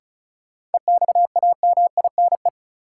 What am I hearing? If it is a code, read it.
Morse code, heard as a keyed tone, about 36 wpm, 700 Hz. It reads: EXAMINE
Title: Sentences from Top 1600 Words - New Word Drill Alt - 35wpm